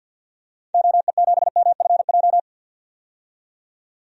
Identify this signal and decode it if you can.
Morse code, heard as a keyed tone, about 52 wpm, 700 Hz. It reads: OE7MFJ
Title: DXCC-100 Call Signs - 6 Characters - 50wpm